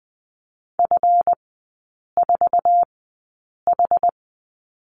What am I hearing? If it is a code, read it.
Morse code, heard as a keyed tone, about 20 wpm, spaced as for 8 wpm, 700 Hz. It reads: F4H
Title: 138 Lesson 23e - Letter f - Three Characters 20wpm